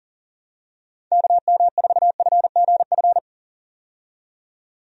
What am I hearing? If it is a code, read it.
Morse code, heard as a keyed tone, about 40 wpm, 700 Hz. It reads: KM4FGF
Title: USA Call Signs - 6 Characters - 40wpm